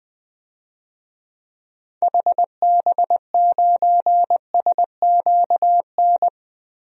Morse code, heard as a keyed tone, about 20 wpm, 700 Hz. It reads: HB9SQN
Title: DXCC-100 Call Signs - 6 Characters - Repeated 2x - 20wpm